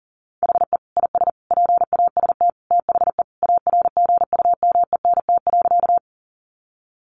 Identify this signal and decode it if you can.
Morse code, heard as a keyed tone, about 40 wpm, 700 Hz. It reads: HE IS PAST THE ARGUMENT.
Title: Sentences from Top 1400 Words 40wpm